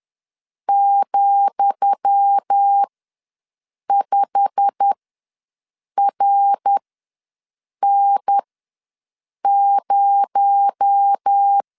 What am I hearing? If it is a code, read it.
Morse code, heard as a keyed tone, about 10 wpm, spaced as for 6 wpm, 800 Hz. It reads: ,5RN0
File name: LEZIONI / 10 SIGNS / 10D COMA ws 10wpm 4314